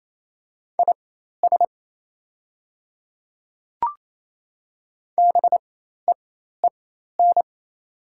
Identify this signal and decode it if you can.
Morse code, heard as a keyed tone, about 28 wpm, spaced as for 13 wpm, 700 Hz. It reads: IS BEEN
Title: 4x Character Spacing - Top 100 Words - 28wpm